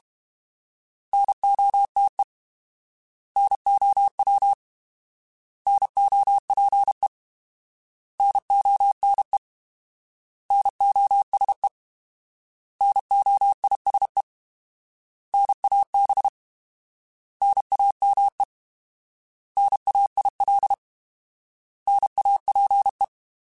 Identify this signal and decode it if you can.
Morse code, heard as a keyed tone, about 32 wpm, 800 Hz. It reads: NOTE NOW NOPE NONE NOSE NOISE NAB NAME NAIL NAPE
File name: Alphabetical / Alphabetical mp3s / Ns @30